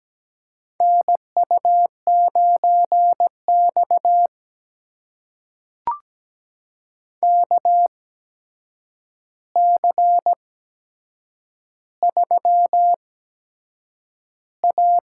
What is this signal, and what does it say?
Morse code, heard as a keyed tone, about 17 wpm, spaced as for 7 wpm, 700 Hz. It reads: NU9X KC3A
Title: USA Call Signs - 8x Character Spacing - 17wpm